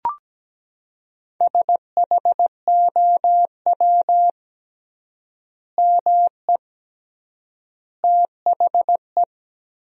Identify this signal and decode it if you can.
Morse code, heard as a keyed tone, about 17 wpm, 700 Hz. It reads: SHOW ME THE
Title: 3x Word-Spacing - Sentences from Top 200 Words - 17wpm